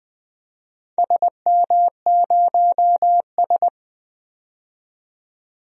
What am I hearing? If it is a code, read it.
Morse code, heard as a keyed tone, about 21 wpm, 700 Hz. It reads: SM0S
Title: DXCC-100 Call Signs - 4 Characters - 20wpm